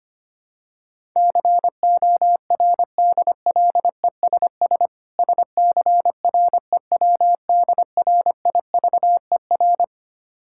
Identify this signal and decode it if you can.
Morse code, heard as a keyed tone, about 25 wpm, 690 Hz. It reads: CORDLESS SCREWDRIVER
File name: practice / 25wpm / Difficult Words with Hints - 25wpm